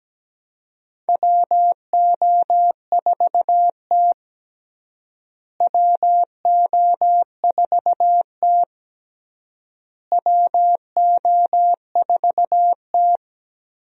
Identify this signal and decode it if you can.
Morse code, heard as a keyed tone, about 17 wpm, 700 Hz. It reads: WO4T WO4T WO4T
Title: USA Call Signs - 4 Characters - Repeated 3x - 17wpm